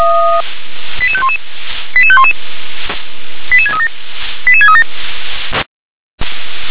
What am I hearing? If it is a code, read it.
Five-tone selective call (ZVEI), heard as quick runs of five tones, tones of about 70 ms. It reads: 88511 88511 88637 88637
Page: ZVEI